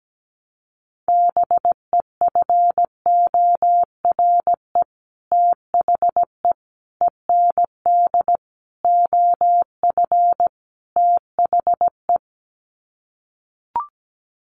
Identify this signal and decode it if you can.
Morse code, heard as a keyed tone, about 17 wpm, 700 Hz. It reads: BEFORE THE END OF THE
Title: Sets of 5 Words Encore 17wpm